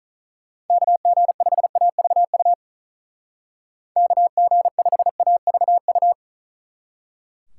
Morse code, heard as a keyed tone, about 45 wpm, 700 Hz. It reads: KG5AVU KG5AVU